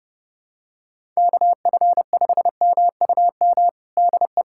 Morse code, heard as a keyed tone, about 32 wpm, 700 Hz. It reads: KF5MUM DE
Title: USA Callsign DE Callsign - 6 Characters - 30wpm